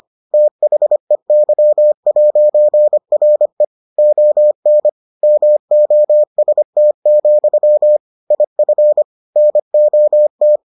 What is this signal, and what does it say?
Morse code, heard as a keyed tone, about 25 wpm, 600 Hz. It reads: THEY'RE ON MOST, IF NOT